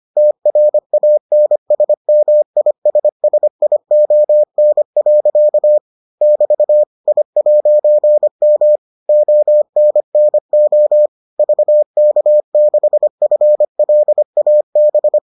Morse code, heard as a keyed tone, about 25 wpm, 600 Hz. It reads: TRANSMISSION. <BT> I'M ONNO VK6FLAB